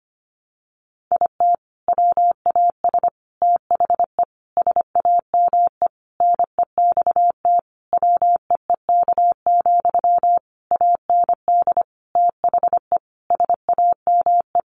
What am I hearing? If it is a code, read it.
Morse code, heard as a keyed tone, about 25 wpm, 700 Hz. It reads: IT WAS THE SAME NEXT WEEK, AND THE SAME
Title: Sentences from Top 500 Words 25wpm